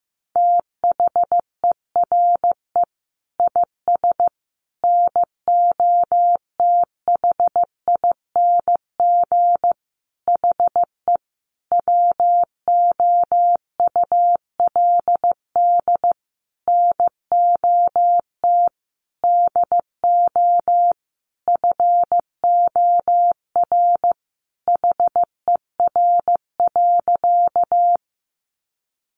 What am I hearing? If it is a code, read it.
Morse code, heard as a keyed tone, about 15 wpm, 700 Hz. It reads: THERE IS NOTHING HE WOULD NOT DO FOR HER.
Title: Sentences from Top 600 Words 15wpm